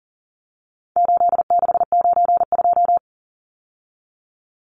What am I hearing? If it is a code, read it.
Morse code, heard as a keyed tone, about 40 wpm, 700 Hz. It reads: 8692
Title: Numbers - 4-Digit 40wpm